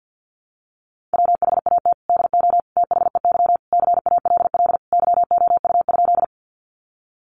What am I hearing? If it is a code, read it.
Morse code, heard as a keyed tone, about 50 wpm, 700 Hz. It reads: WHAT DO THEY CALL YOU?
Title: Sentences from Top 100 Words 50wpm